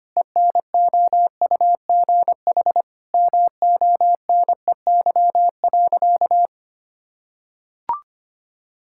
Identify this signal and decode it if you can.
Morse code, heard as a keyed tone, about 25 wpm, 700 Hz. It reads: ENOUGH MONEY.
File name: practice / 25wpm / Sentences with 4-Word Phrases - 25wpm